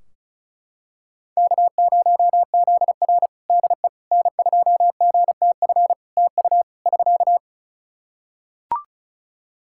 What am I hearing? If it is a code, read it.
Morse code, heard as a keyed tone, about 36 wpm, 700 Hz. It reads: K0ZR DE N2GTF TU <SK>
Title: Practice QSO - Exchange 4 - Ending - 35wpm